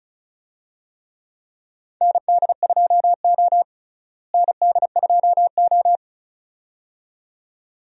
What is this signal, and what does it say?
Morse code, heard as a keyed tone, about 34 wpm, 700 Hz. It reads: ND2O ND2O
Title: USA Call Signs - 4 Characters - Repeated 2x - 35wpm